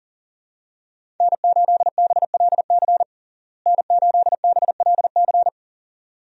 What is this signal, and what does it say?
Morse code, heard as a keyed tone, about 40 wpm, 700 Hz. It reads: N8BLC N8BLC
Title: USA Call Signs - 5 Characters - Repeated 2x - 40wpm